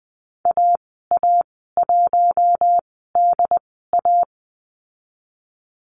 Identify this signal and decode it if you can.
Morse code, heard as a keyed tone, about 20 wpm, spaced as for 14 wpm, 700 Hz. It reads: AA1DA
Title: 069 Lesson 12g - Letter d - Call Signs 20wpm